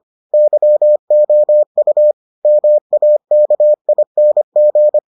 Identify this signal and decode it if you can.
Morse code, heard as a keyed tone, about 25 wpm, 600 Hz. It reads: YOU MAKING